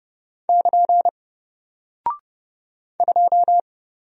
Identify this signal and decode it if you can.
Morse code, heard as a keyed tone, about 31 wpm, 700 Hz. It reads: <KN> 2
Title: Single Character (Letter:Number:Punctuation:Prosign) 30wpm